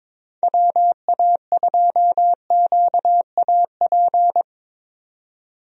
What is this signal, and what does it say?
Morse code, heard as a keyed tone, about 21 wpm, 700 Hz. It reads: WA2QAP
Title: USA Call Signs - 6 Characters - 22wpm